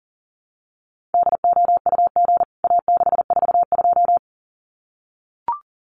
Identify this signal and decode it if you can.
Morse code, heard as a keyed tone, about 40 wpm, 700 Hz. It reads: DOUG A642
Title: CWT Exchange 40wpm